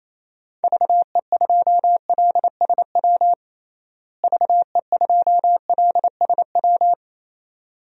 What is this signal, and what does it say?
Morse code, heard as a keyed tone, about 28 wpm, 700 Hz. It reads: VE2LSW VE2LSW
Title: DXCC-100 Call Signs - 6 Characters - Repeated 2x - 28wpm